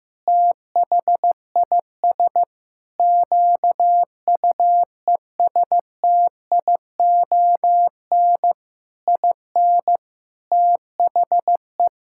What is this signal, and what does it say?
Morse code, heard as a keyed tone, about 15 wpm, 700 Hz. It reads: THIS QUESTION IN THE
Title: Sentences from Top 1000 Words - Review Part 6 - 15wpm